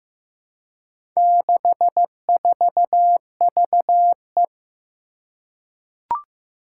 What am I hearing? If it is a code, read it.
Morse code, heard as a keyed tone, about 15 wpm, 700 Hz. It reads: 64VE
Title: Four Challenging Letter-Number Combinations 15wpm